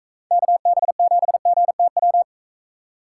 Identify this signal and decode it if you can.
Morse code, heard as a keyed tone, about 44 wpm, 700 Hz. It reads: KB7GTW